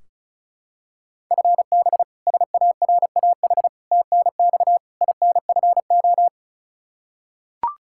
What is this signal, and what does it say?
Morse code, heard as a keyed tone, about 36 wpm, 700 Hz. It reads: FB SARAH TNX INFO
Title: K1USN SST QSO Practice 35wpm